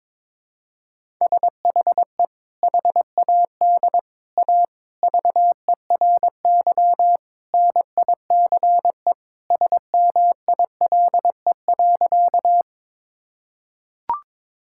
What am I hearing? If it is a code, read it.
Morse code, heard as a keyed tone, about 22 wpm, 700 Hz. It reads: SHE HAD A VERY NICE SMILE.
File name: practice / 22wpm / Sentences with 2-Word Phrases - 22wpm